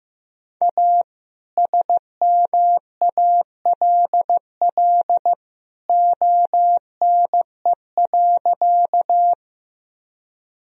Morse code, heard as a keyed tone, about 15 wpm, 700 Hz. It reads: A SMALL ONE.
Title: Sentences from Top 1000 Words - Review Part 1 - 15wpm